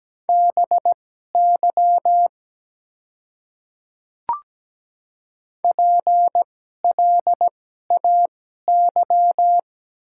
Morse code, heard as a keyed tone, about 17 wpm, spaced as for 12 wpm, 700 Hz. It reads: BY PLAY
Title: 2x Character Spacing - Top 200 Words - 17wpm